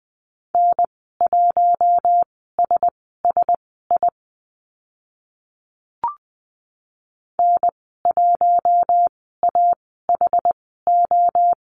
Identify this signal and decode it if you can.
Morse code, heard as a keyed tone, about 20 wpm, spaced as for 15 wpm, 700 Hz. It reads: N1SSI N1AHO
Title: 076 Lesson 13g - Letter L - Call Signs 20wpm